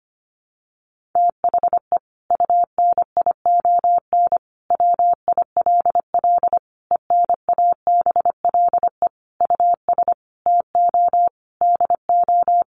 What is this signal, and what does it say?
Morse code, heard as a keyed tone, about 25 wpm, 700 Hz. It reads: THE UNION WILL ENABLE US TO DO
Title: Sentences from Top 1500 Words 25wpm